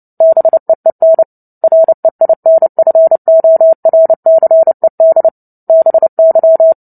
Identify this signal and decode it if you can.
Morse code, heard as a keyed tone, about 30 wpm, 650 Hz. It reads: BEEN REINFORCED BY
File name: Book Two / 03-Chapter 3 - 29 WPM